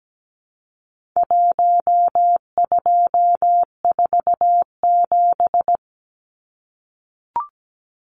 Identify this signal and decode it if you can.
Morse code, heard as a keyed tone, about 17 wpm, 700 Hz. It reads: 1247
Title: Numbers - 4-Digit 17wpm